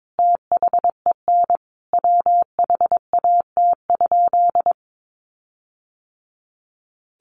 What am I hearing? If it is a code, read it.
Morse code, heard as a keyed tone, about 22 wpm, 700 Hz. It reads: THEN WHAT?
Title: Sentences from Top 1300 Words 22wpm